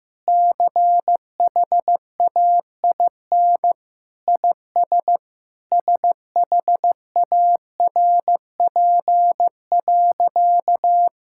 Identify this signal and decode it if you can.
Morse code, heard as a keyed tone, about 15 wpm, 700 Hz. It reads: CHAIN IS SHARP.
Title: Sentences from Top 1600 Words 15wpm